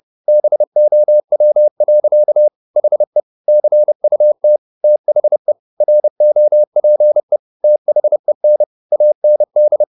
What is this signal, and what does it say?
Morse code, heard as a keyed tone, about 30 wpm, 600 Hz. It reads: BOW. HE CUT THE ROPE THEN AND